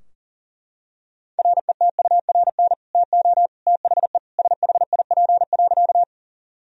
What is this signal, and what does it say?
Morse code, heard as a keyed tone, about 40 wpm, 700 Hz. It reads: RETURN TO THE SHIP.
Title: Sentences from Top 1000 Words - Review Part 2 - 40wpm